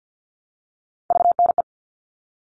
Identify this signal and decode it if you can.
Morse code, heard as a keyed tone, about 50 wpm, 700 Hz. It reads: VNE
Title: All Cut Numbers - 3-Digit 50wpm